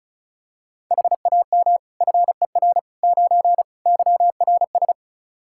Morse code, heard as a keyed tone, about 36 wpm, 700 Hz. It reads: HAM FER 9 YRS
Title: K1USN SST QSO Practice 35wpm